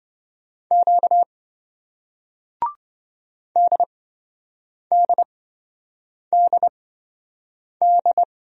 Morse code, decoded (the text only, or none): Q DDDD